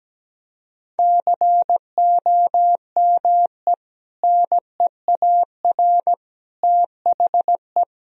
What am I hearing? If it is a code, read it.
Morse code, heard as a keyed tone, about 17 wpm, 700 Hz. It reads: COME NEAR THE
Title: Sentences from Top 1000 Words - Review Part 1 - 17wpm